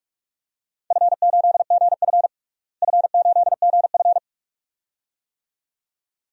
Morse code, heard as a keyed tone, about 42 wpm, 700 Hz. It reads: F8GF F8GF
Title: DXCC-100 Call Signs - 4 Characters - Repeated 2x - 45wpm